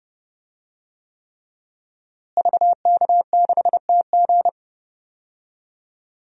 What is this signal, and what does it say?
Morse code, heard as a keyed tone, about 32 wpm, 700 Hz. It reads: VK6TG
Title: DXCC-100 Call Signs - 5 Characters - Repeated 3x - 30wpm